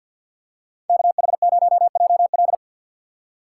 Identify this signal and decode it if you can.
Morse code, heard as a keyed tone, about 52 wpm, 700 Hz. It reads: KH0JL
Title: USA Call Signs - 4x Character Spacing - 50wpm